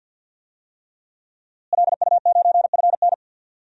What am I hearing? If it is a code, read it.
Morse code, heard as a keyed tone, about 51 wpm, 700 Hz. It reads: LU9FN
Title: DXCC-100 Call Signs - 5 Characters - Repeated 3x - 50wpm